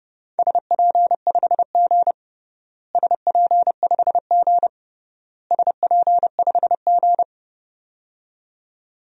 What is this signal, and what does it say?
Morse code, heard as a keyed tone, about 31 wpm, 700 Hz. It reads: SP5G SP5G SP5G
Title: DXCC-100 Call Signs - 4 Characters - Repeated 3x - 30wpm